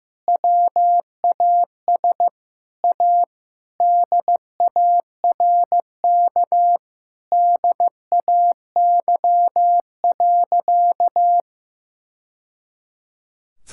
Morse code, heard as a keyed tone, about 15 wpm, 700 Hz. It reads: WAS A DARK DAY.
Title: Sentences from Top 1000 Words - Review Part 5 - 15wpm